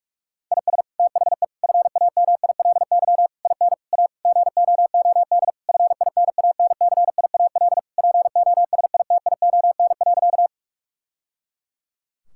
Morse code, heard as a keyed tone, about 46 wpm, 700 Hz. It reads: IS THE FAMILY IN A GOOD FINANCIAL POSITION.
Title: Sentences from Top 1000 Words - Review Part 5 - 45wpm